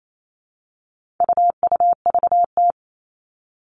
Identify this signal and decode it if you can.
Morse code, heard as a keyed tone, about 28 wpm, 700 Hz. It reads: UUVT
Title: All Cut Numbers - 4-Digit 28wpm